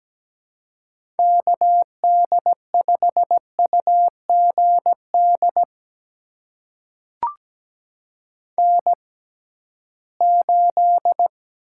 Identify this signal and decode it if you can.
Morse code, heard as a keyed tone, about 17 wpm, spaced as for 13 wpm, 700 Hz. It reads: KD5UGD N8
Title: USA Call Signs - 6x Character Spacing - 17wpm